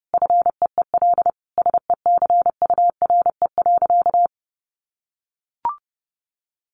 Morse code, heard as a keyed tone, about 30 wpm, 700 Hz. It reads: FEEL SECURE.
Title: Sentences from Top 1500 Words 30wpm